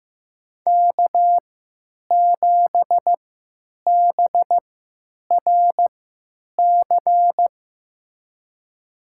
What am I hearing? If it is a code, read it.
Morse code, heard as a keyed tone, about 15 wpm, spaced as for 9 wpm, 700 Hz. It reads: K7BRC
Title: USA Call Signs - 3x Character Spacing - 15wpm